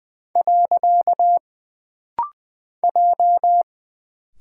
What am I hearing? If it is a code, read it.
Morse code, heard as a keyed tone, about 20 wpm, 700 Hz. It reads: . J
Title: Single Character (Letter:Number:Punctuation:Prosign) - Rapid-Fire - 20wpm